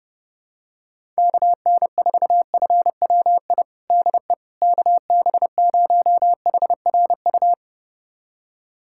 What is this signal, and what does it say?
Morse code, heard as a keyed tone, about 30 wpm, 700 Hz. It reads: KN4FWI DE KB0HRU
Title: USA Callsign DE Callsign - 6 Characters - 30wpm